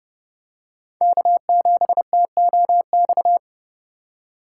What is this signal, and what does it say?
Morse code, heard as a keyed tone, about 29 wpm, 700 Hz. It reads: K7TOX